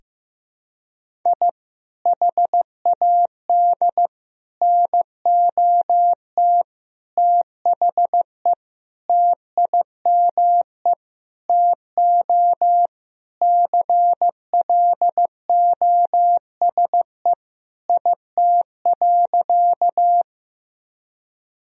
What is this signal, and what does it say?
Morse code, heard as a keyed tone, about 15 wpm, 700 Hz. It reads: I HAD NOT THE TIME TO CLOSE IT.
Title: Sentences from Top 1000 Words - Review Part 1 - 15wpm